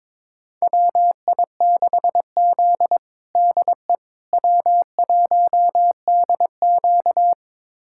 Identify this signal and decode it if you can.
Morse code, heard as a keyed tone, about 22 wpm, 700 Hz. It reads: WI6Z DE W1DQ
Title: USA Callsign DE Callsign - 4 Characters - 22wpm